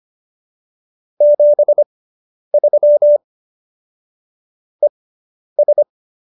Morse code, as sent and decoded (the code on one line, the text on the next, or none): --... ...-- / . ...
73 ES